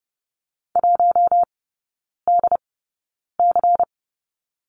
Morse code, heard as a keyed tone, about 30 wpm, spaced as for 9 wpm, 690 Hz. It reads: 1DC
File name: Lesson 16e - Letter c - Three Characters 30wpm